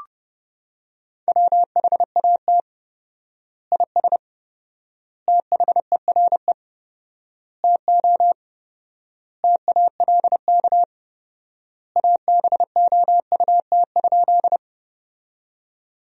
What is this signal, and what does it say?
Morse code, heard as a keyed tone, about 30 wpm, 700 Hz. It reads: WHAT IS THERE TO TALK ABOUT?
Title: 4x Word-Spacing - Sentences from Top 400 Words - 30wpm